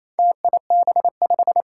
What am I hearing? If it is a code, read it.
Morse code, heard as a keyed tone, about 28 wpm, 700 Hz. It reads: TIB5
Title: Four Challenging Letter-Number Combinations 28wpm